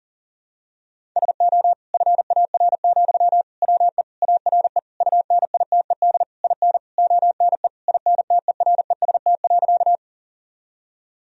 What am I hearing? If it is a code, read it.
Morse code, heard as a keyed tone, about 40 wpm, 700 Hz. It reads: SO FAR, WE ARE UNITED IN ONE INTEREST.
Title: Sentences from Top 1000 Words - Review Part 4 - 40wpm